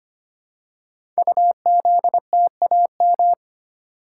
Morse code, heard as a keyed tone, about 25 wpm, 700 Hz. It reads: UZTAM
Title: Code Groups - 5 Letters - 25wpm